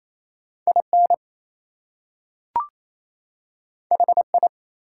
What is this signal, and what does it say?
Morse code, heard as a keyed tone, about 28 wpm, 700 Hz. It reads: IN HI
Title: Two Letter Words - 28wpm